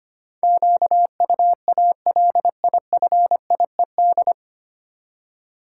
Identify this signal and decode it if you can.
Morse code, heard as a keyed tone, about 25 wpm, 700 Hz. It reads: QUALIFIED
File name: practice / 25wpm / 9 Letter Words - First 3 Letters Given - 25wpm